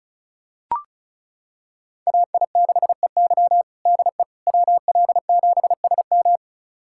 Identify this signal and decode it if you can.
Morse code, heard as a keyed tone, about 34 wpm, 700 Hz. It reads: AI6EY DE WL7SM
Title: USA Callsign DE Callsign - 5 Characters - 35wpm